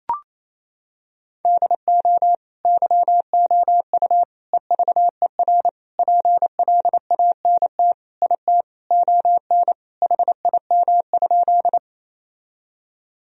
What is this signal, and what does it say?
Morse code, heard as a keyed tone, about 28 wpm, 700 Hz. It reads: DO YOU EVER PLANT IT ON HIM?
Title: Sentences from Top 1000 Words - Review Part 1 - 28wpm